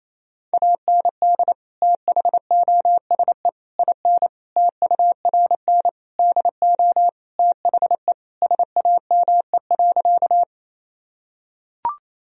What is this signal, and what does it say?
Morse code, heard as a keyed tone, about 28 wpm, 700 Hz. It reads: AND THOSE IN TURN DO THE SAME.
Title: Sentences from Top 1000 Words - Review Part 1 - 28wpm